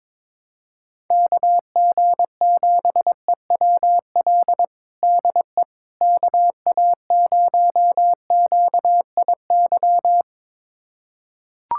Morse code, heard as a keyed tone, about 22 wpm, 700 Hz. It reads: KG7EWL DE KA0QIY
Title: USA Callsign DE Callsign - 6 Characters - 22wpm